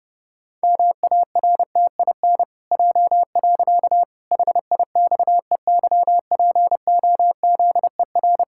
Morse code, heard as a keyed tone, about 30 wpm, 700 Hz. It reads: MARTIN J. HIXEYPOZER
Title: Pangram Challenge v2 - 30wpm